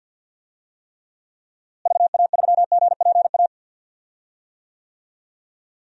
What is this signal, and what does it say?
Morse code, heard as a keyed tone, about 52 wpm, 700 Hz. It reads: VA3GPA
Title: DXCC-100 Call Signs - 6 Characters - Repeated 2x - 50wpm